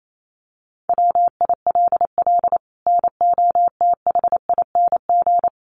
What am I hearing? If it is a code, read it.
Morse code, heard as a keyed tone, about 28 wpm, 700 Hz. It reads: WILL NOTHING
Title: Sentences from Top 1200 Words 28wpm